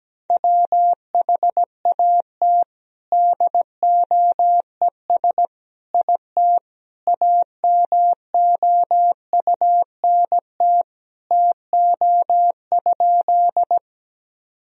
Morse code, heard as a keyed tone, about 17 wpm, 700 Hz. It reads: WHAT DOES IT AMOUNT TO?
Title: Sentences from Top 1000 Words - Review Part 4 - 17wpm